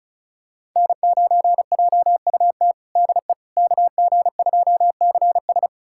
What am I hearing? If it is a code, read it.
Morse code, heard as a keyed tone, about 36 wpm, 700 Hz. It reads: N9JUT DE KG2CS
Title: USA Callsign DE Callsign - 5 Characters - 35wpm